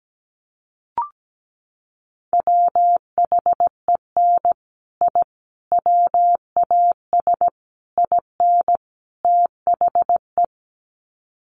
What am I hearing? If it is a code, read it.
Morse code, heard as a keyed tone, about 17 wpm, 700 Hz. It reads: WHEN I WAS IN THE
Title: Sets of 5 Words Encore2 17wpm